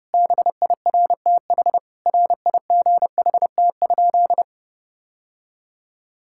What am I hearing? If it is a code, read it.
Morse code, heard as a keyed tone, about 30 wpm, 700 Hz. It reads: BIRTH RIGHT?
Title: Sentences from Top 1000 Words - Review Part 5 - 30wpm